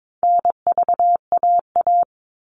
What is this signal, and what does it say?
Morse code, heard as a keyed tone, about 22 wpm, 700 Hz. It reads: NVAA
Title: All Cut Numbers - 4-Digit 22wpm